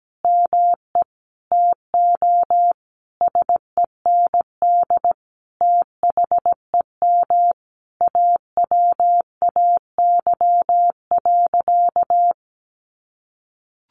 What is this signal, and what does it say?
Morse code, heard as a keyed tone, about 17 wpm, 700 Hz. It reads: ME TO SEND THEM AWAY.
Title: Sentences from Top 1500 Words 17wpm